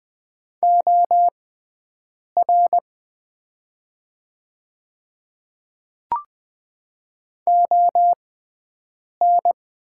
Morse code, heard as a keyed tone, about 20 wpm, spaced as for 7 wpm, 700 Hz. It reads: OR ON
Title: 6x Character Spacing - Top 100 Words - 20wpm